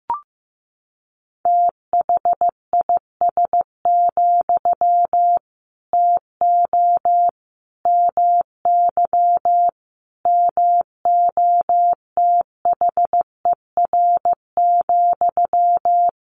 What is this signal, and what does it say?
Morse code, heard as a keyed tone, about 15 wpm, 700 Hz. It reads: THIS, TO MY MOTHER,
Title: Sentences from Top 1500 Words 15wpm